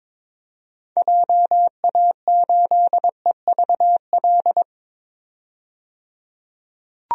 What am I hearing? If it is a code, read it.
Morse code, heard as a keyed tone, about 22 wpm, 700 Hz. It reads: JA8EVL
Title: DXCC-100 Call Signs - 6 Characters - 22wpm